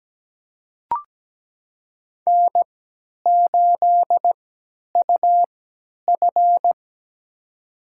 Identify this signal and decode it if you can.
Morse code, heard as a keyed tone, about 17 wpm, spaced as for 10 wpm, 700 Hz. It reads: N8UF
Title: USA Call Signs - 3x Character Spacing - 17wpm